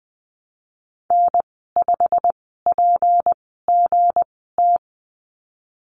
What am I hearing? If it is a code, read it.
Morse code, heard as a keyed tone, about 20 wpm, spaced as for 14 wpm, 700 Hz. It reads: N5PGT
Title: 161 Lesson 26g - Letter g - Call Signs 20wpm